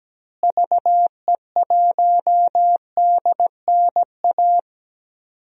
Morse code, heard as a keyed tone, about 17 wpm, 700 Hz. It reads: VE1DNA
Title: DXCC-100 Call Signs - 6 Characters - Repeated 3x - 17wpm